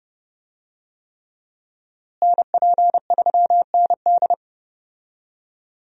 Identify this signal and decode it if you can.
Morse code, heard as a keyed tone, about 31 wpm, 700 Hz. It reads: NP3ND
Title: USA Call Signs - 5 Characters - Repeated 3x - 30wpm